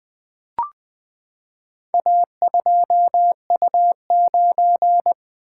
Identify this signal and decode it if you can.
Morse code, heard as a keyed tone, about 20 wpm, 700 Hz. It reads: A2U9
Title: Four Letter-Number Combinations - 20wpm